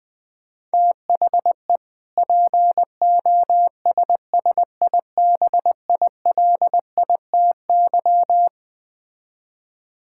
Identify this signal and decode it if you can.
Morse code, heard as a keyed tone, about 20 wpm, 700 Hz. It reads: THE POSSIBILITY
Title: Build A Phrase - 3-Word Phrases - 20wpm